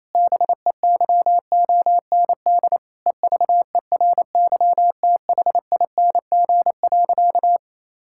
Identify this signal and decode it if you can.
Morse code, heard as a keyed tone, about 28 wpm, 700 Hz. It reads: BEYOND EVERYTHING.
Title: Sentences from Top 1000 Words - Review Part 6 - 28wpm